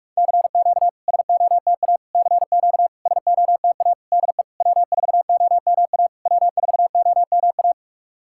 Morse code, heard as a keyed tone, about 46 wpm, 690 Hz. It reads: CQ SOTA CQ SOTA DE W4OMA W4OMA
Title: SOTA QSO Practice 45wpm